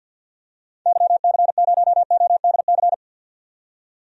Morse code, heard as a keyed tone, about 50 wpm, 700 Hz. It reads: YC0ODC
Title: DXCC-100 Call Signs - 6 Characters - 50wpm